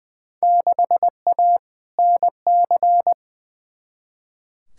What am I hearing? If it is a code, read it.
Morse code, heard as a keyed tone, about 20 wpm, 700 Hz. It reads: 6A NC
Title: Field Day Exchanges Encore - 20wpm